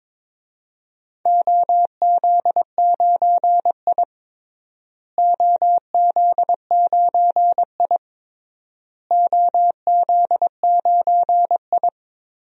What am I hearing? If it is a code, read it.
Morse code, heard as a keyed tone, about 22 wpm, 700 Hz. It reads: OZ9I OZ9I OZ9I
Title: DXCC-100 Call Signs - 4 Characters - Repeated 3x - 22wpm